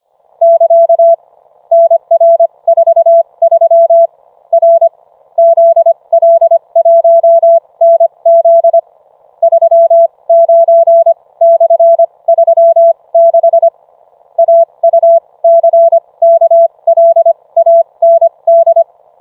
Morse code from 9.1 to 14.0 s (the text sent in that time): 39/36